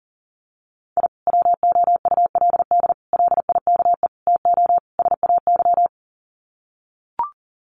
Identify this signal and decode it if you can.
Morse code, heard as a keyed tone, about 40 wpm, 700 Hz. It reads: I WOULD LIKE TO SAY
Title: Sets of 5 Words Encore2 40wpm